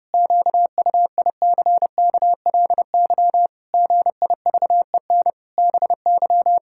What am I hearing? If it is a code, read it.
Morse code, heard as a keyed tone, about 32 wpm, 700 Hz. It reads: QUICKLY GIVEN BY
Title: Pangram Challenge v2 - 30wpm